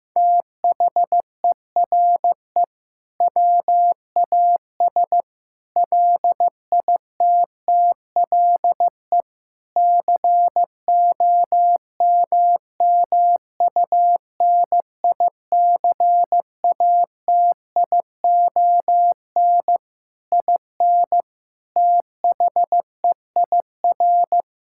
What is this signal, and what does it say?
Morse code, heard as a keyed tone, about 15 wpm, 700 Hz. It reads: THERE WAS LITTLE COMMUNICATION IN THEIR
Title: Sentences from Top 1000 Words - Review Part 6 - 15wpm